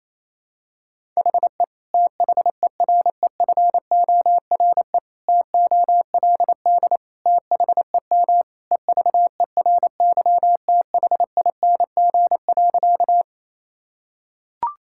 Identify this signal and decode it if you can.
Morse code, heard as a keyed tone, about 28 wpm, 700 Hz. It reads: HE THEREFORE TOLD THEM EVERYTHING.
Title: Sentences from Top 1000 Words - Review Part 5 - 28wpm